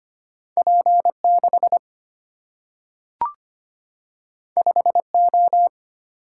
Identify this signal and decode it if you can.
Morse code, heard as a keyed tone, about 25 wpm, 690 Hz. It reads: P6 5O
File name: Two Letter-Number Combinations - 25wpm